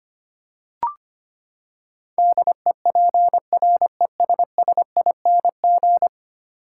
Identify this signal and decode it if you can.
Morse code, heard as a keyed tone, about 25 wpm, 700 Hz. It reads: DEPRESSING
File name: Ten Letter Words - 25wpm